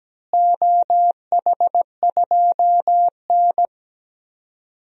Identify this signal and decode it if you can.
Morse code, heard as a keyed tone, about 17 wpm, 700 Hz. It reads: OH2N
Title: DXCC-100 Call Signs - 4 Characters - Repeated 2x - 17wpm